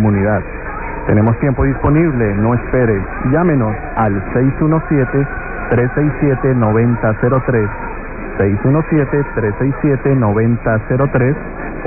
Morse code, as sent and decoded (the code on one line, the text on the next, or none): none